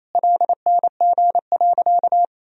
Morse code, heard as a keyed tone, about 28 wpm, 700 Hz. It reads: LNG.
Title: Sentences from Top 500 Words and Abbreviations 28wpm